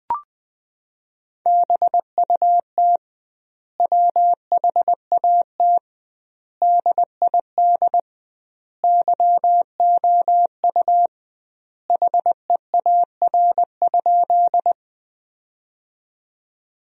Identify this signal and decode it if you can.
Morse code, heard as a keyed tone, about 20 wpm, 700 Hz. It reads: BUT WHAT DID YOU HEAR?
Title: 2x Word-Spacing - Sentences from Top 300 Words - 20wpm